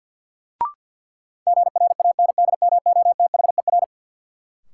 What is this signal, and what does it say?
Morse code, heard as a keyed tone, about 52 wpm, 700 Hz. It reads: GRANDMOTHER
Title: Sentences from Top 1200 Words - New Word Drill - 50wpm